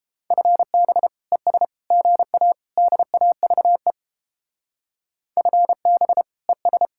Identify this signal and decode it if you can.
Morse code, heard as a keyed tone, about 29 wpm, 700 Hz. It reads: FB ES GA DAVE FB ES